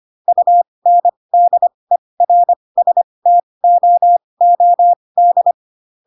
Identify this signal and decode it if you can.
Morse code, heard as a keyed tone, about 25 wpm, spaced as for 20 wpm, 700 Hz. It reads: UNDERSTOOD
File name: Pinocchio / Pinocchio ch8 15 25 ik1hge